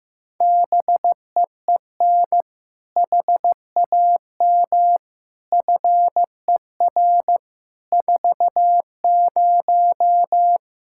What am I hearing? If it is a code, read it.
Morse code, heard as a keyed tone, about 15 wpm, 700 Hz. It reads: BEEN HAM FER 40
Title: Practice QSO - Exchange 3 - More Info - 15wpm